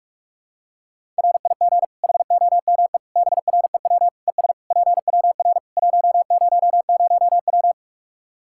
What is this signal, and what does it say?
Morse code, heard as a keyed tone, about 46 wpm, 700 Hz. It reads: RIG HOME BREW ES PWR 100W
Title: Practice QSO - Exchange 2 - Info - 45wpm